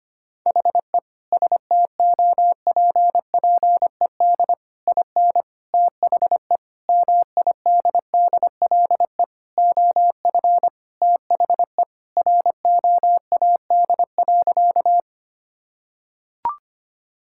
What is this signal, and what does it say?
Morse code, heard as a keyed tone, about 25 wpm, 700 Hz. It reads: HE STOPPED IN THE MIDDLE OF THE ROAD.
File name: practice / 25wpm / Sentences with 3-Word Phrases - 25wpm